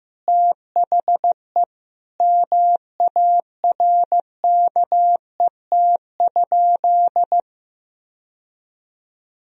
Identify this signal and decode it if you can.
Morse code, heard as a keyed tone, about 15 wpm, 700 Hz. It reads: THE MARKET?
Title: Sentences from Top 1000 Words - Review Part 3 - 15wpm